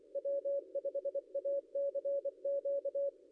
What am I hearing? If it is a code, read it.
Morse code, heard as a keyed tone, about 25 wpm, 560 Hz. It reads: W5ACQ
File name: contests / ss-cw-k5tr-op-n5rz -0800